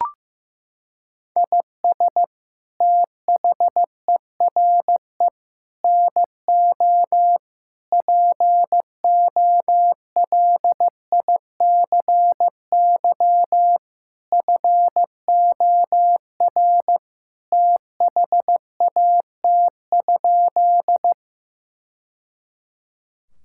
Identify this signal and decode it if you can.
Morse code, heard as a keyed tone, about 15 wpm, 700 Hz. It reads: IS THERE NO POLICY FOR THAT?
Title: Sentences from Top 1000 Words - Review Part 5 - 15wpm